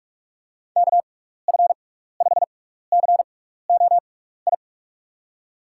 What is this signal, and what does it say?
Morse code, heard as a keyed tone, about 44 wpm, spaced as for 15 wpm, 700 Hz. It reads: KF5COI
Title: USA Call Signs - 6 Characters - 6x Character Spacing - 45wpm